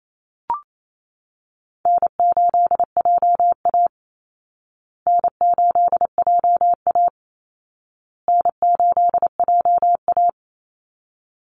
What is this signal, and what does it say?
Morse code, heard as a keyed tone, about 28 wpm, 700 Hz. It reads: N8JA N8JA N8JA
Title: Call Signs Repeated 2x - 28wpm